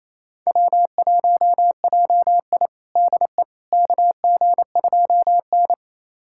Morse code, heard as a keyed tone, about 28 wpm, 700 Hz. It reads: W1JI DE KG2N
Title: USA Callsign DE Callsign - 4 Characters - 28wpm